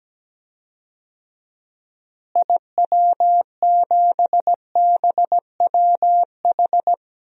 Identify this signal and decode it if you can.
Morse code, heard as a keyed tone, about 17 wpm, 700 Hz. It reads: IW7BWH
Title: DXCC-100 Call Signs - 6 Characters - Repeated 2x - 17wpm